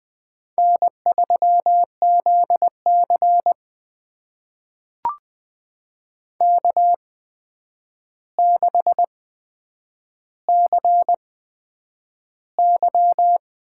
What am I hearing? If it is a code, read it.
Morse code, heard as a keyed tone, about 20 wpm, spaced as for 9 wpm, 700 Hz. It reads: N3ZC K6CY
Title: USA Call Signs - 4 Characters - 8x Character Spacing - 20wpm